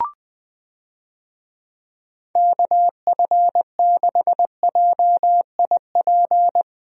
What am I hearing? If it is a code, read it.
Morse code, heard as a keyed tone, about 20 wpm, 700 Hz. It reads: KF6JIP
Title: USA Call Signs - 6 Characters - Repeated 3x - 20wpm